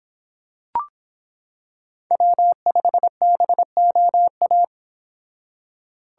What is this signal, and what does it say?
Morse code, heard as a keyed tone, about 25 wpm, 700 Hz. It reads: W5BOA